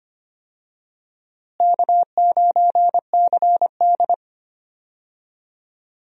Morse code, heard as a keyed tone, about 25 wpm, 700 Hz. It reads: K9CD
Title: USA Call Signs - 4 Characters - Repeated 2x - 25wpm